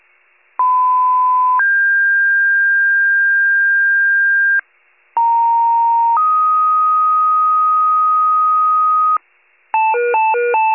Two-tone page, heard as a one-second tone, then a three-second tone, about 1010 then 1600 Hz, about 930 then 1250 Hz, with noise. It is heard from 0.6 to 9.2 s.